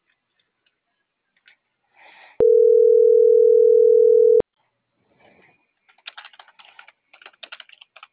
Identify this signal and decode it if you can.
Ringback tone (North America), 440 + 480 Hz.